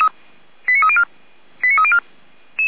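Five-tone selective call (ZVEI), heard as quick runs of five tones, tones of about 70 ms. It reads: 89304 89304
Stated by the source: ZVEI five-tone call 89305 received by SDR, archived by audio.f85360.feuerwehr-orlamuende.org fire brigade.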